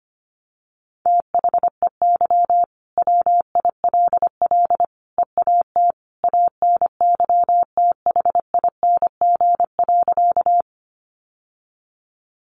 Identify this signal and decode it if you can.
Morse code, heard as a keyed tone, about 25 wpm, 700 Hz. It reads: THEY WILL EAT ANYTHING.
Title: Sentences from Top 1000 Words 25wpm